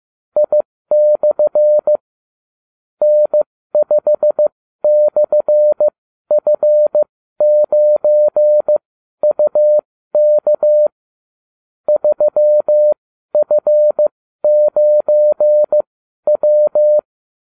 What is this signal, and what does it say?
Morse code, heard as a keyed tone, about 15 wpm, spaced as for 12 wpm, 600 Hz. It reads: I/ N5/F9UK 3F9W
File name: cw / ubacw29 2